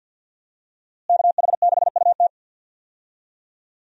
Morse code, heard as a keyed tone, about 53 wpm, 700 Hz. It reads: KH6UT
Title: USA Call Signs - 5 Characters - Repeated 2x - 50wpm